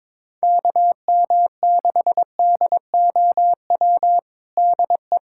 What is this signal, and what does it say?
Morse code, heard as a keyed tone, about 21 wpm, 700 Hz. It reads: KM6DOW DE
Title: USA Callsign DE Callsign - 6 Characters - 22wpm